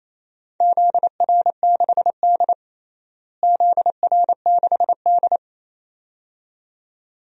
Morse code, heard as a keyed tone, about 27 wpm, 700 Hz. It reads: ZR6D ZR6D
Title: DXCC-100 Call Signs - 4 Characters - Repeated 2x - 28wpm